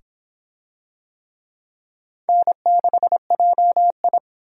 Morse code, heard as a keyed tone, about 22 wpm, 700 Hz. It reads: N6JI